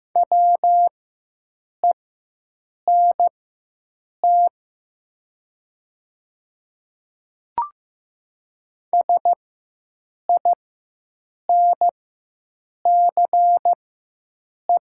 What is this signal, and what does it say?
Morse code, heard as a keyed tone, about 15 wpm, spaced as for 7 wpm, 700 Hz. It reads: WENT SINCE
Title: 4x Character Spacing - Top 300 Words - 15wpm